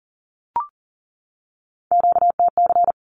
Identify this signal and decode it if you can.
Morse code, heard as a keyed tone, about 40 wpm, 700 Hz. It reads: QTC
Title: Q-Codes 40wpm